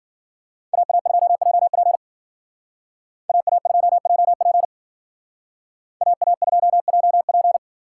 Morse code, heard as a keyed tone, about 52 wpm, 700 Hz. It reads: AA2JP AA2JP AA2JP